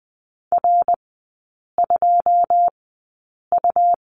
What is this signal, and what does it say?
Morse code, heard as a keyed tone, about 20 wpm, spaced as for 8 wpm, 690 Hz. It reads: R2U
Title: 145 Lesson 24e - Letter y - Three Characters 20wpm